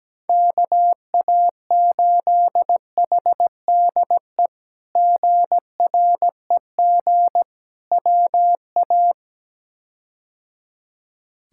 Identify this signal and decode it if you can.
Morse code, heard as a keyed tone, about 17 wpm, 700 Hz. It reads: KA8HDE GREG WA
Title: K1USN SST QSO Practice - 17wpm